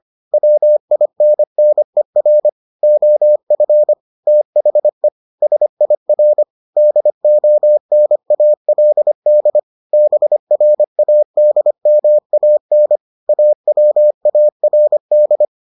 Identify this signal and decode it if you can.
Morse code, heard as a keyed tone, about 25 wpm, 600 Hz. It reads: WINNER OF THE SIR DONALD BRADMAN AWARD